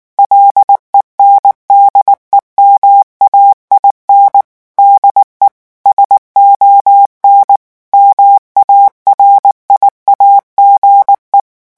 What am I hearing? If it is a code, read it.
Morse code, heard as a keyed tone, about 19 wpm, 800 Hz. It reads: LENDEMAIN DE SON MARIAGE
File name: morse mp3 / revision9 citations humoristiques vitesse 18 mots minute